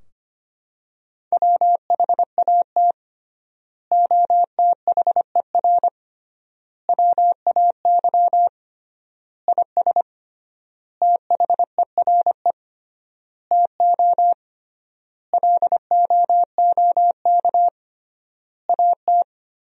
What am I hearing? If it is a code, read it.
Morse code, heard as a keyed tone, about 25 wpm, 700 Hz. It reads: WHAT OTHER WAY IS THERE TO LOOK AT
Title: 3x Word-Spacing - Sentences from Top 100 Words - 25wpm